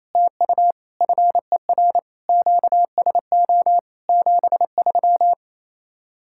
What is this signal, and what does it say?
Morse code, heard as a keyed tone, about 33 wpm, 700 Hz. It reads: TU FER QSO 73
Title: Speed-Racing Common QSO Phrases Encore - 28wpm